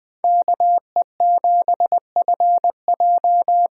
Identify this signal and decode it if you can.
Morse code, heard as a keyed tone, about 20 wpm, 700 Hz. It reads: KE7FJ
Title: USA Call Signs - 6 Characters - 6x Character Spacing - 20wpm